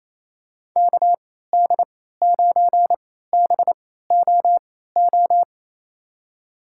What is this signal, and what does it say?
Morse code, heard as a keyed tone, about 27 wpm, spaced as for 16 wpm, 700 Hz. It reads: KD9BOO
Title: USA Call Signs - 6 Characters - 3x Character Spacing - 28wpm